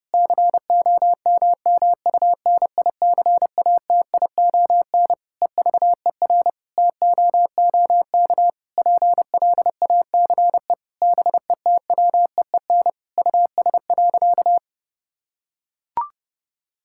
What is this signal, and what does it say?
Morse code, heard as a keyed tone, about 30 wpm, 700 Hz. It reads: COMMUNICATION EVER TOOK PLACE BETWEEN US.
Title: Sentences from Top 1000 Words - Review Part 6 - 30wpm